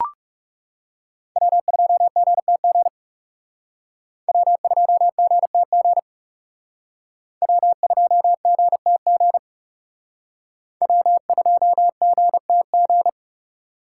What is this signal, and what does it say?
Morse code, heard as a keyed tone, about 36 wpm, 700 Hz. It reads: W2GTG W2GTG W2GTG W2GTG
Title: Speed Racing - USA 1x3 Call Signs 30wpm